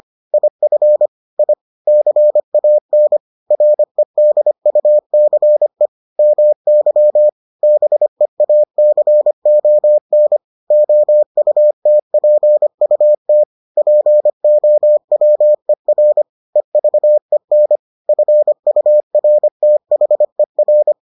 Morse code, heard as a keyed tone, about 25 wpm, 600 Hz. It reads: IF I CAN REDUCE MY BEACON OUTPUT POWER EVEN FURTHER